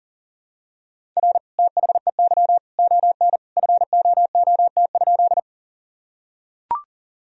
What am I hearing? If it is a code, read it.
Morse code, heard as a keyed tone, about 40 wpm, 700 Hz. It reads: R THEY ON FOOT?
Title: Sentences from Top 500 Words and Abbreviations 40wpm